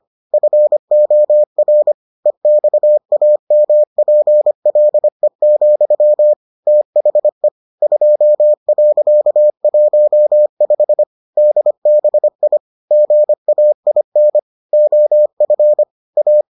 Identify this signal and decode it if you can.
Morse code, heard as a keyed tone, about 25 wpm, 600 Hz. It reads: FOR EXAMPLE, THE 2.15 DBI GAIN OF A